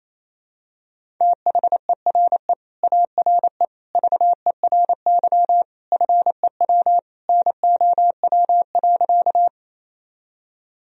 Morse code, heard as a keyed tone, about 27 wpm, 700 Hz. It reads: THERE ARE VERY FEW NOW.
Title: Sentences from Top 1000 Words - Review Part 1 - 28wpm